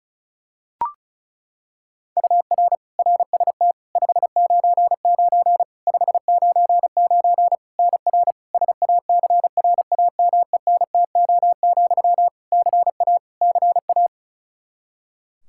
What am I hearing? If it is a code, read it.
Morse code, heard as a keyed tone, about 36 wpm, 700 Hz. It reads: UR RST 599 599 NR SACRAMENTO, CA CA
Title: K1USN SST QSO Practice 35wpm